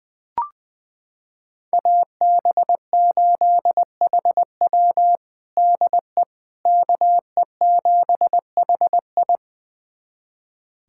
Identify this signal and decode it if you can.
Morse code, heard as a keyed tone, about 20 wpm, 700 Hz. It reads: AB8HW DE KE7HI
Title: USA Callsign DE Callsign - 5 Characters - 20wpm